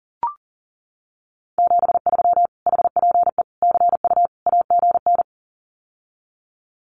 Morse code, heard as a keyed tone, about 40 wpm, 700 Hz. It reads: 73 HPE CU AGN
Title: Common QSO Phrases Encore 40wpm